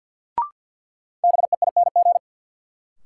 Morse code, heard as a keyed tone, about 52 wpm, 700 Hz. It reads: BEING
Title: Sets of 4 Words - New Word Drill - 50wpm